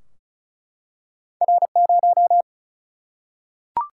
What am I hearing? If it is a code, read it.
Morse code, heard as a keyed tone, about 36 wpm, 700 Hz. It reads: R0
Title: Single-Letter Single-Number 35wpm